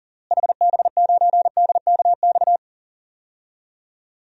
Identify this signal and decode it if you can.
Morse code, heard as a keyed tone, about 40 wpm, 700 Hz. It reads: HB9DKX
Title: DXCC-100 Call Signs - 6 Characters - Repeated 3x - 40wpm